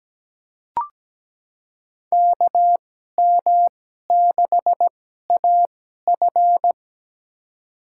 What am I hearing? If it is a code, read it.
Morse code, heard as a keyed tone, about 17 wpm, spaced as for 12 wpm, 700 Hz. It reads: KM6AF
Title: USA Call Signs - 2x Character Spacing - 17wpm